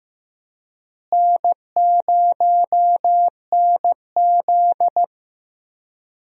Morse code, heard as a keyed tone, about 15 wpm, 700 Hz. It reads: N0NZ
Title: USA Call Signs - 4 Characters - Repeated 2x - 15wpm